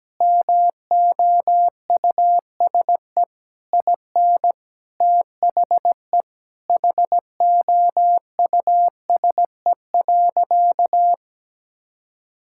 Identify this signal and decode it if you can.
Morse code, heard as a keyed tone, about 17 wpm, 700 Hz. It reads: MOUSE IN THE HOUSE.
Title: Sentences from Top 1000 Words - Review Part 5 - 17wpm